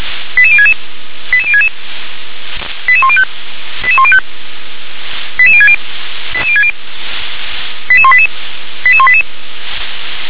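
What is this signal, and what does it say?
Five-tone selective call (ZVEI), heard as quick runs of five tones, tones of about 70 ms. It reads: 88066 88066 88105 88105 88060 88060 88188 88188
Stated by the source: Five-tone ZVEI selective call received on an SDR.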